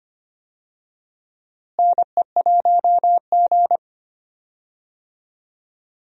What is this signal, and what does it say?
Morse code, heard as a keyed tone, about 25 wpm, 700 Hz. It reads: NE1G
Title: USA Call Signs - 4 Characters - Repeated 2x - 25wpm